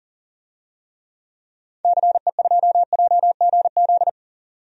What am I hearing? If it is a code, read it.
Morse code, heard as a keyed tone, about 39 wpm, 700 Hz. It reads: CE2JGZ
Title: DXCC-100 Call Signs - 6 Characters - 40wpm